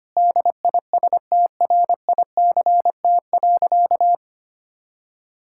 Morse code, heard as a keyed tone, about 25 wpm, 700 Hz. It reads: DISTRICT.
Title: Sentences from Top 1600 Words 25wpm